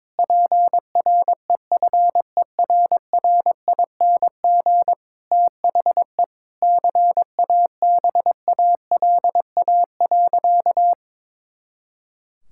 Morse code, heard as a keyed tone, about 22 wpm, 700 Hz. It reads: PREFERRING THE CABALA.
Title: Pangram Challenge v2 - 22wpm